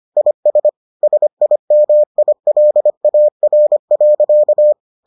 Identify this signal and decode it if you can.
Morse code, heard as a keyed tone, about 25 wpm, 600 Hz. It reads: IS SIMILAR.